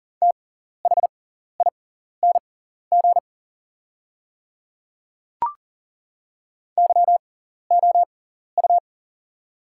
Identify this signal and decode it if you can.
Morse code, heard as a keyed tone, about 40 wpm, spaced as for 14 wpm, 700 Hz. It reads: THING YOU
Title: 6x Character Spacing - Top 500 Words - 40wpm